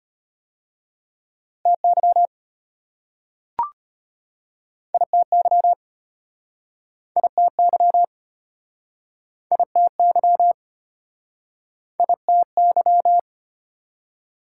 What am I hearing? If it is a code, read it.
Morse code, heard as a keyed tone, about 33 wpm, 700 Hz. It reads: TY ITY ITY ITY ITY